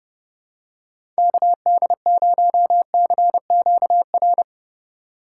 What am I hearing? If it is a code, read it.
Morse code, heard as a keyed tone, about 30 wpm, 700 Hz. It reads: KD0CQR
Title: USA Call Signs - 6 Characters - 2x Character Spacing - 30wpm